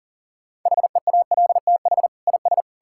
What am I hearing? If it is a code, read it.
Morse code, heard as a keyed tone, about 40 wpm, 700 Hz. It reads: HEALTH IS
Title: Sentences from Top 1000 Words - Review Part 5 - 40wpm